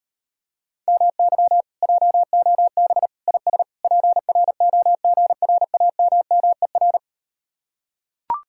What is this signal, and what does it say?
Morse code, heard as a keyed tone, about 38 wpm, 700 Hz. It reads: MY JOB IS PROGRAMMER